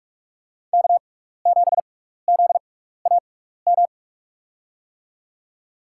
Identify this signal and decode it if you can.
Morse code, heard as a keyed tone, about 46 wpm, spaced as for 15 wpm, 700 Hz. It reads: K7ZAM
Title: USA Call Signs - 5 Characters - 6x Character Spacing - 45wpm